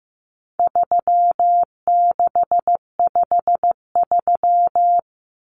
Morse code, heard as a keyed tone, about 15 wpm, 700 Hz. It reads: 3653
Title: Common Cut Numbers - 4-Digit 15wpm